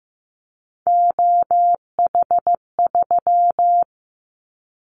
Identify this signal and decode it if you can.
Morse code, heard as a keyed tone, about 15 wpm, 700 Hz. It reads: OH3
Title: Callsign Prefixes for Top 50 DXCC - 15wpm